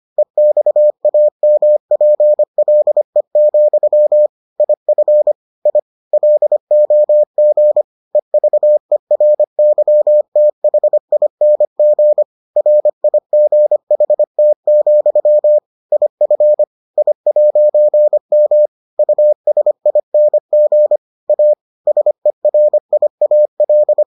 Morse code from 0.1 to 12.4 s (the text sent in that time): EXAMPLE, IF I LOG EVERYTHING